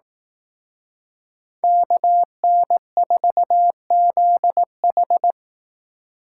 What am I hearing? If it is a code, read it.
Morse code, heard as a keyed tone, about 18 wpm, 700 Hz. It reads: KN4ZH
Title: Speed Racing - USA 2x2 Call Signs 15wpm